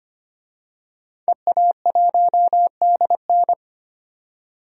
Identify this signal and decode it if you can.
Morse code, heard as a keyed tone, about 25 wpm, 700 Hz. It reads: EA1DN
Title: DXCC-100 Call Signs - 5 Characters - 25wpm